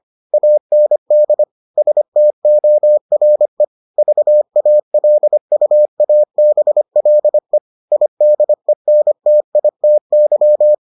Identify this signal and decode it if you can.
Morse code, heard as a keyed tone, about 25 wpm, 600 Hz. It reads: AND STORE VALUABLE IDENTITY